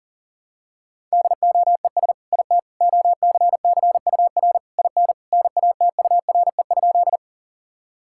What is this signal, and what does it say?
Morse code, heard as a keyed tone, about 40 wpm, 700 Hz. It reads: DOES IT OCCUR IN NATURE?
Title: Sentences from Top 1000 Words - Review Part 6 - 40wpm